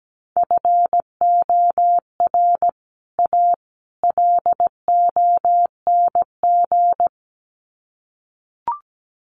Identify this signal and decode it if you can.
Morse code, heard as a keyed tone, about 17 wpm, 700 Hz. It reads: FOR A LONG
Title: Sets of 3 Words 17wpm